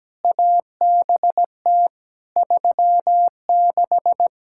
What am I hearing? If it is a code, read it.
Morse code, heard as a keyed tone, about 17 wpm, 700 Hz. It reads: ABT 36
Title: Practice QSO - Exchange 3 - More Info - 17wpm